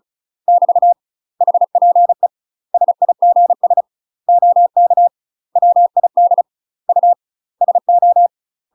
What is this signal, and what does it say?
Morse code, heard as a keyed tone, about 36 wpm, 700 Hz. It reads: <BT> HPE SIGS OK WID U SO
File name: UK MASTER QSOs TRAINING / 35WPM / UK MASTER QSOs 0016